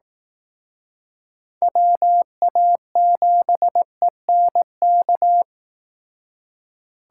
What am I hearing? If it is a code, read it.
Morse code, heard as a keyed tone, about 18 wpm, 700 Hz. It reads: WA7ENK
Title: Speed Racing - USA 2x3 Call Signs 15wpm